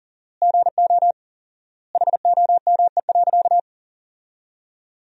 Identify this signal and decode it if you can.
Morse code, heard as a keyed tone, about 40 wpm, 700 Hz. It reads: GO HOME.
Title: 4x Word-Spacing - Sentences from Top 300 Words - 40wpm